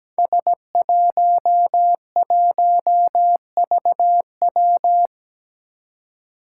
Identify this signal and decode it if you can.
Morse code, heard as a keyed tone, about 17 wpm, 700 Hz. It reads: S11VW
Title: Five Challenging Letter-Number Combinations 17wpm